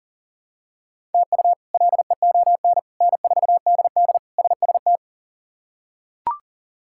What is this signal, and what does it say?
Morse code, heard as a keyed tone, about 40 wpm, 700 Hz. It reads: TU LEON N4DD SST
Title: K1USN SST QSO Practice - 40wpm